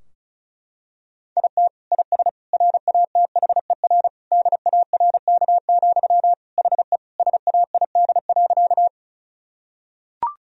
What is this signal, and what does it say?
Morse code, heard as a keyed tone, about 36 wpm, 700 Hz. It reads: IT IS RATHER DARK, HE SAID.
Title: Sentences from Top 1000 Words - Review Part 4 - 35wpm